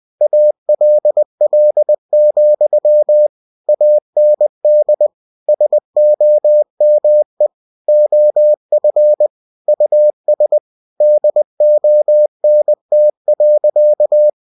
Morse code, heard as a keyed tone, about 20 wpm, 600 Hz. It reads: ALL, AND SOME OF US DONT.
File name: cw-books / winnie-the-pooh / winnie-20wpm-page-0045